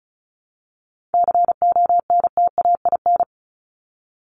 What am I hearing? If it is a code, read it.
Morse code, heard as a keyed tone, about 34 wpm, 700 Hz. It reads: CONTAIN
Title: ING Suffix - Top 100 Words with Base Words 35wpm